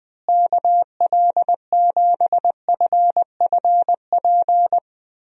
Morse code, heard as a keyed tone, about 21 wpm, 700 Hz. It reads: KL7FFP
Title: USA Call Signs - 6 Characters - Repeated 3x - 20wpm